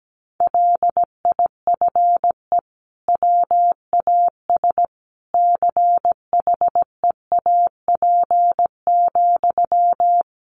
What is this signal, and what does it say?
Morse code, heard as a keyed tone, about 17 wpm, 700 Hz. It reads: LIFE WAS CHEAP,
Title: Sentences from Top 1200 Words 17wpm